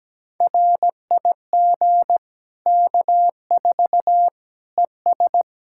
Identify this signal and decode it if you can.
Morse code, heard as a keyed tone, about 17 wpm, 700 Hz. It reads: RIG K4 ES
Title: Practice QSO - Exchange 2 - Info - 17wpm